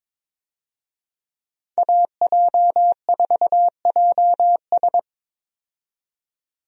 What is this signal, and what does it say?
Morse code, heard as a keyed tone, about 22 wpm, 700 Hz. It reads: AJ4JS
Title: USA Call Signs - 5 Characters - 22wpm